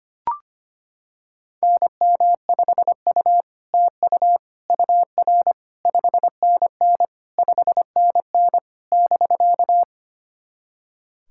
Morse code, heard as a keyed tone, about 25 wpm, 700 Hz. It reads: NM5U TU UR 5NN 5NN <BK>
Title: SOTA QSO Practice 25wpm